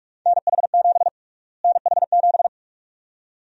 Morse code, heard as a keyed tone, about 46 wpm, 700 Hz. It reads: NH7 NH7
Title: USA Call Sign Prefixes - Repeated 2x - 45wpm